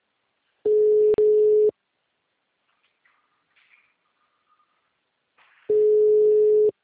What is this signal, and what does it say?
Ringback tone (Europe, 425 Hz).
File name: out- -2040- - - .11315